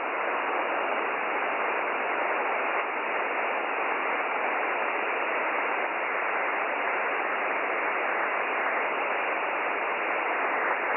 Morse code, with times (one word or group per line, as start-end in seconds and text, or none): none